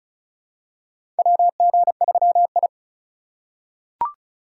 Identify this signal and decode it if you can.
Morse code, heard as a keyed tone, about 34 wpm, 700 Hz. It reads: WG3I
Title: USA Call Signs - 4 Characters - 2x Character Spacing - 35wpm